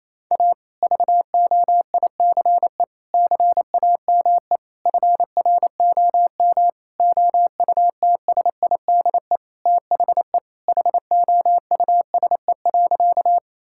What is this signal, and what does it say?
Morse code, heard as a keyed tone, about 28 wpm, 700 Hz. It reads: A VOICE CAME FROM OUTSIDE THE HOUSE.
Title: Sentences from Top 1000 Words - Review Part 6 - 28wpm